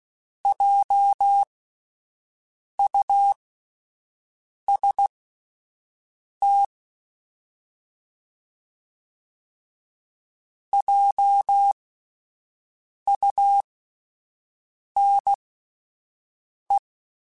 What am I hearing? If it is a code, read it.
Morse code, heard as a keyed tone, about 16 wpm, spaced as for 5 wpm, 800 Hz. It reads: JUST JUNE